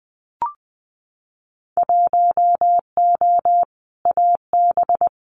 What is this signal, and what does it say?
Morse code, heard as a keyed tone, about 20 wpm, 700 Hz. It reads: 1O AB
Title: Winter Field Day Exchanges Encore 20wpm